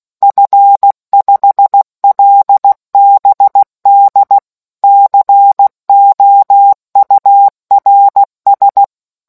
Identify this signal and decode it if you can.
Morse code, heard as a keyed tone, about 16 wpm, 800 Hz. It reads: F5LBD COURS